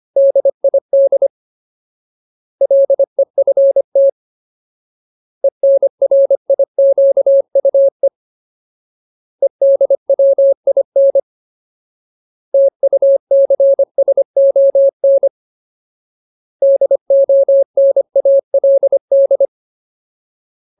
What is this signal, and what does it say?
Morse code, heard as a keyed tone, about 25 wpm, 550 Hz. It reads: DID LEFT ENRIQUE EDWIN TUCSON DONALD
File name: Ultimate Word List / 25WPM Ultimate